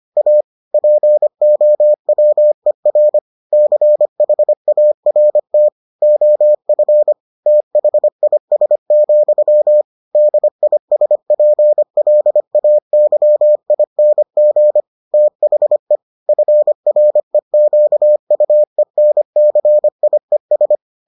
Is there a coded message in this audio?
Yes: Morse code from 0.2 to 20.8 s, heard as a keyed tone, about 25 wpm, 600 Hz. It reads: A POWER CHART OF THIS, DISPLAYING THE FREQUENCIES